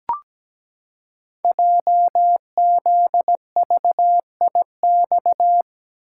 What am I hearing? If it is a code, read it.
Morse code, heard as a keyed tone, about 17 wpm, 700 Hz. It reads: JZVIX
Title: Code Groups - 5 Letters - 17wpm